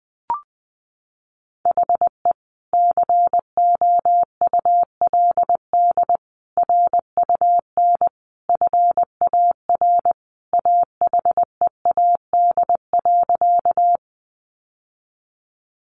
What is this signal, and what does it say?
Morse code, heard as a keyed tone, about 20 wpm, 700 Hz. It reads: HE COULD RUN FAR AHEAD.
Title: Sentences from Top 1100 Words 20wpm